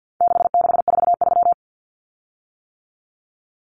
Morse code, heard as a keyed tone, about 50 wpm, 700 Hz. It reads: 6643
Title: Common Cut Numbers - 4-Digit 50wpm